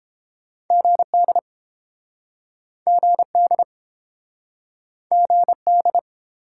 Morse code, heard as a keyed tone, about 29 wpm, 700 Hz. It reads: GD GD GD